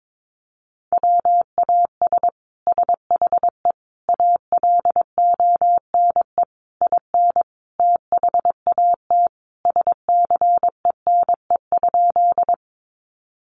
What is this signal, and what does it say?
Morse code, heard as a keyed tone, about 22 wpm, 700 Hz. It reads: WAS SHE ALONE IN THAT SCENE?
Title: Sentences from Top 1100 Words 22wpm